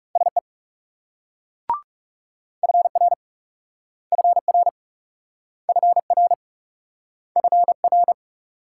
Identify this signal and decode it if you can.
Morse code, heard as a keyed tone, about 38 wpm, 700 Hz. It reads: SE FR FR FR FR